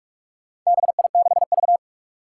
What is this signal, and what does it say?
Morse code, heard as a keyed tone, about 44 wpm, 700 Hz. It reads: BI6V
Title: Four Challenging Letter-Number Combinations 45wpm